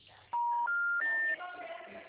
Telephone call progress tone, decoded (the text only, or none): special information tone